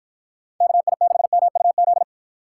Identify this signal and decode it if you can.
Morse code, heard as a keyed tone, about 53 wpm, 700 Hz. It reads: KI6MUZ